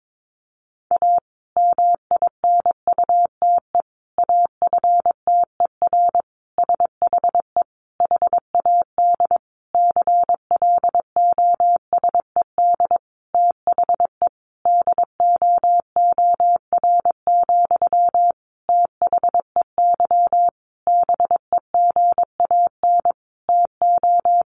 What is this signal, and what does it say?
Morse code, heard as a keyed tone, about 22 wpm, 700 Hz. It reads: A MINUTE AFTER SHE HAD CLOSED THE DOOR, THEY BEGAN TO
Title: Sentences from Top 1400 Words 22wpm